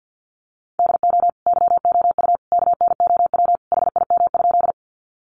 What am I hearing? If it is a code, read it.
Morse code, heard as a keyed tone, about 50 wpm, 700 Hz. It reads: DO YOU KNOW HIM?
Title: Sentences from Top 100 Words 50wpm